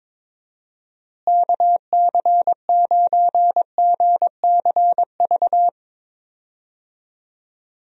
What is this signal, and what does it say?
Morse code, heard as a keyed tone, about 22 wpm, 700 Hz. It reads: KC9GCV
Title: USA Call Signs - 6 Characters - Repeated 2x - 22wpm